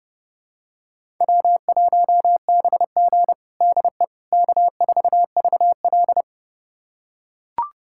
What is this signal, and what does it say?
Morse code, heard as a keyed tone, about 32 wpm, 700 Hz. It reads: W1BG DE K4VL
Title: USA Callsign DE Callsign - 4 Characters - 30wpm